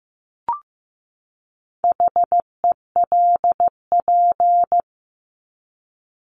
Morse code, heard as a keyed tone, about 15 wpm, 700 Hz. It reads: HELP
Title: Top 200 Words 15wpm